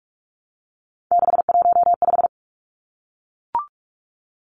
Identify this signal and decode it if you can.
Morse code, heard as a keyed tone, about 44 wpm, 700 Hz. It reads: 615
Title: Numbers - 3-Digit 45wpm